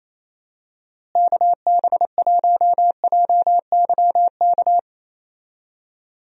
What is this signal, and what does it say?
Morse code, heard as a keyed tone, about 27 wpm, 700 Hz. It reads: KB1JYK
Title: USA Call Signs - 6 Characters - Repeated 2x - 28wpm